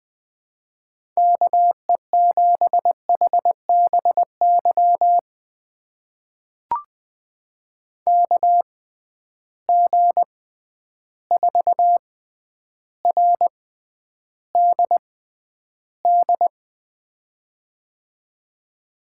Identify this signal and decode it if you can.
Morse code, heard as a keyed tone, about 21 wpm, spaced as for 10 wpm, 700 Hz. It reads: KE7HBY KG4RDD
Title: USA Call Signs - 6 Characters - 6x Character Spacing - 20wpm